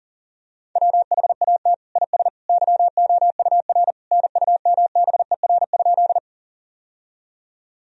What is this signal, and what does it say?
Morse code, heard as a keyed tone, about 40 wpm, 700 Hz. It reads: WHAT IS YOUR NUMBER?
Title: Sentences from Top 1000 Words - Review Part 1 - 40wpm